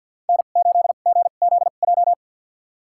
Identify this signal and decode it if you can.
Morse code, heard as a keyed tone, about 52 wpm, spaced as for 36 wpm, 700 Hz. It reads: N8GZJ
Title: USA Call Signs - 5 Characters - 2x Character Spacing - 50wpm